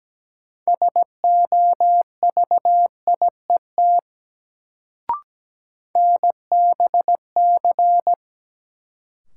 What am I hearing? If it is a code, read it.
Morse code, heard as a keyed tone, about 17 wpm, 700 Hz. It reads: SOVIET NBC
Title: Sets of 5 Words Encore - New Word Drill - 17wpm